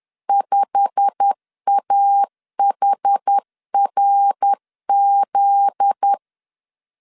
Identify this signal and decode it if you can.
Morse code, heard as a keyed tone, about 10 wpm, 800 Hz. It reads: 5AHRZ
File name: LEZIONI / 8 YJ45 / 8 YJ45 10wpm 2549